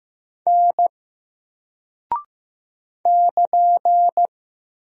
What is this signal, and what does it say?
Morse code, heard as a keyed tone, about 15 wpm, 700 Hz. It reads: N <KN>
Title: Single Character (Letter:Number:Punctuation:Prosign) 15wpm